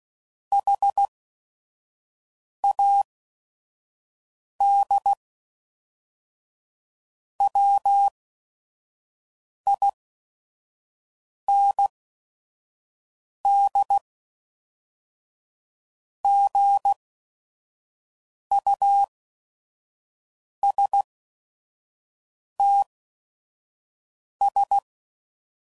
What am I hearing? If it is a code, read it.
Morse code, heard as a keyed tone, about 16 wpm, spaced as for 5 wpm, 800 Hz. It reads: HAD WIND GUSTS
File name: CW Code Groups @5WPM / Weather Sets / Weather 5 5